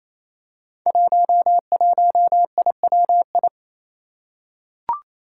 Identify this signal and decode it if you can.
Morse code, heard as a keyed tone, about 27 wpm, 700 Hz. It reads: 11IWI
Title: Five Challenging Letter-Number Combinations 28wpm